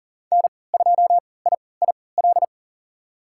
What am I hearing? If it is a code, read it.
Morse code, heard as a keyed tone, about 40 wpm, spaced as for 23 wpm, 700 Hz. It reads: N2IIL
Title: USA Call Signs - 5 Characters - 3x Character Spacing - 40wpm